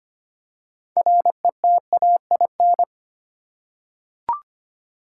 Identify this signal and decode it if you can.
Morse code, heard as a keyed tone, about 25 wpm, 700 Hz. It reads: RETAIN
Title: Six Letter Words - 25wpm